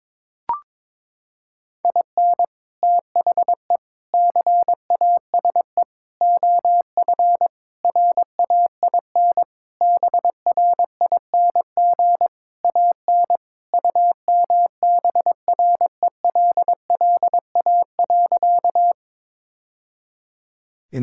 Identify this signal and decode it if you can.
Morse code, heard as a keyed tone, about 22 wpm, 700 Hz. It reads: IN THE CASE OF RAIN BRING AN UMBRELLA.
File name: practice / 22wpm / Sentences with 3-Word Phrases - 22wpm